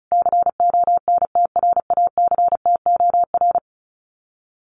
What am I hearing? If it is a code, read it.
Morse code, heard as a keyed tone, about 34 wpm, 700 Hz. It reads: CONTRACTOR
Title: Common QSO Phrases Encore 35wpm